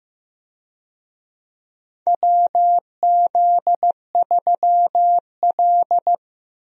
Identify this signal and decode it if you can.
Morse code, heard as a keyed tone, about 15 wpm, 700 Hz. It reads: WZ3L
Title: USA Call Signs - 4 Characters - 15wpm